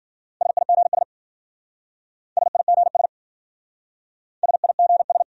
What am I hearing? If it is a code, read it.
Morse code, heard as a keyed tone, about 49 wpm, 700 Hz. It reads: SIGS SIGS SIGS
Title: Speed-Racing Abbreviations - 40wpm